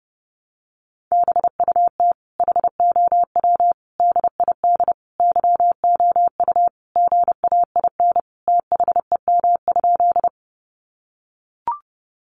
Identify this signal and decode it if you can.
Morse code, heard as a keyed tone, about 30 wpm, 700 Hz. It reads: BUT HOW DID YOU GAIN THEM?
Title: Sentences from Top 1000 Words 30wpm